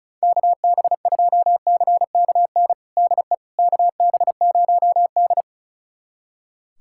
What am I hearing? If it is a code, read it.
Morse code, heard as a keyed tone, about 36 wpm, 700 Hz. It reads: KB2CKN DE KB0D
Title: USA Callsign DE Callsign 35wpm